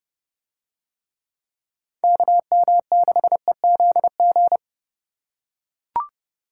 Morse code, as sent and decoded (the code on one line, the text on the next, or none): -.- -- -.... --.. --.
KM6EZG